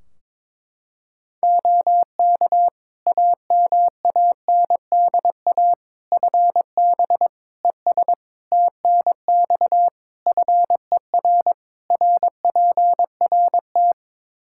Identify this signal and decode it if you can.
Morse code, heard as a keyed tone, about 22 wpm, 700 Hz. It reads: OK AMANDA FB ES TNX FER RPRT
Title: K1USN SST QSO Practice 22wpm